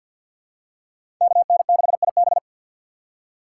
Morse code, heard as a keyed tone, about 48 wpm, 700 Hz. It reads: KN6IB
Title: USA Call Signs - 5 Characters - 6x Character Spacing - 50wpm